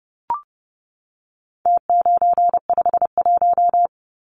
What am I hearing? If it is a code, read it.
Morse code, heard as a keyed tone, about 30 wpm, 700 Hz. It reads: T951
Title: Numbers - Contest Style - 4-Digit 30wpm